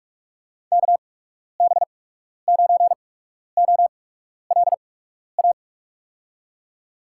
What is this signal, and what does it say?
Morse code, heard as a keyed tone, about 46 wpm, spaced as for 12 wpm, 700 Hz. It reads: KB9OLA
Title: USA Call Signs - 8x Character Spacing - 45wpm